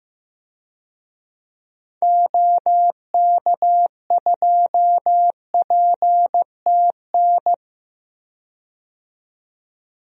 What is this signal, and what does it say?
Morse code, heard as a keyed tone, about 15 wpm, 700 Hz. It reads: OK2PTN
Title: DXCC-100 Call Signs - 6 Characters - 15wpm